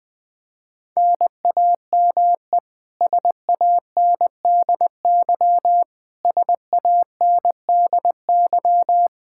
Morse code, heard as a keyed tone, about 21 wpm, 700 Hz. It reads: NAME SANDY SANDY
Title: Practice QSO - Exchange 1 - Beginning - 20wpm